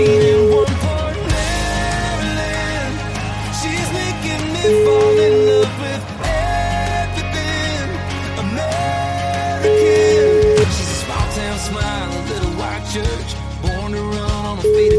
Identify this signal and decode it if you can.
Ringback tone (Europe, 425 Hz), with noise.